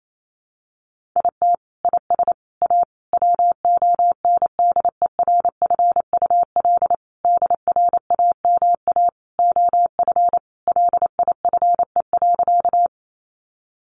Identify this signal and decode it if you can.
Morse code, heard as a keyed tone, about 28 wpm, 700 Hz. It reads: IT IS A WONDERFUL DRAMA OF LIFE.
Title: Sentences from Top 1500 Words 28wpm